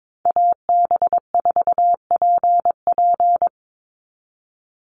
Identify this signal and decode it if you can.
Morse code, heard as a keyed tone, about 22 wpm, 700 Hz. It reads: AB4PP
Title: NA SOTA Chaser Callsigns 22wpm